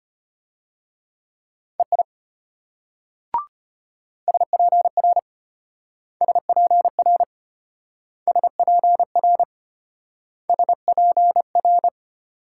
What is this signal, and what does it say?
Morse code, heard as a keyed tone, about 33 wpm, 700 Hz. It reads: EI SPR SPR SPR SPR